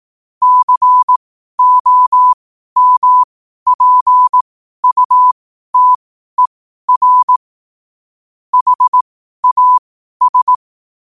Morse code, heard as a keyed tone, about 18 wpm, spaced as for 12 wpm, 1000 Hz. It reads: COMPUTER HAS
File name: qotdfiles / 12wpm